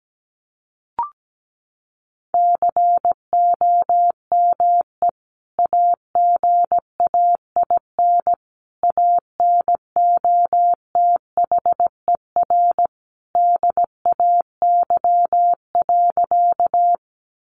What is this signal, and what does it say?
Morse code, heard as a keyed tone, about 17 wpm, 700 Hz. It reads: COME AGAIN ANOTHER DAY.
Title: Sentences from Top 600 Words 17wpm